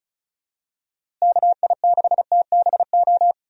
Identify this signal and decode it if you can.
Morse code, heard as a keyed tone, about 36 wpm, 700 Hz. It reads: KI6TBO
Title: USA Call Signs - 6 Characters - Repeated 3x - 35wpm